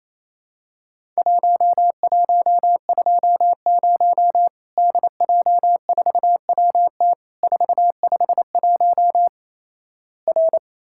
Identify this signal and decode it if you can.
Morse code, heard as a keyed tone, about 28 wpm, 700 Hz. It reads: 1120 DJ4WT 451 R
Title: Worked All Europe DX Contest - QTC Exchange - 28wpm